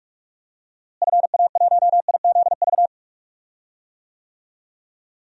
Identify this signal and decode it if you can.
Morse code, heard as a keyed tone, about 46 wpm, 700 Hz. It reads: FA1IZV
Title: DXCC-100 Call Signs - 6 Characters - Repeated 3x - 45wpm